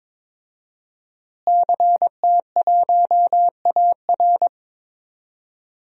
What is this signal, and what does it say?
Morse code, heard as a keyed tone, about 22 wpm, 700 Hz. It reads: CT1AR
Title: DXCC-100 Call Signs - 5 Characters - Repeated 3x - 22wpm